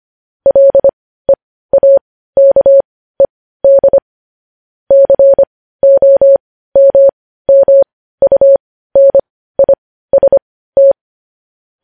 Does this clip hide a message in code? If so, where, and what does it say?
Yes: Morse code from 0.5 to 10.9 s, heard as a keyed tone, about 25 wpm, spaced as for 15 wpm, 550 Hz. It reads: LEAKED COMMUNIST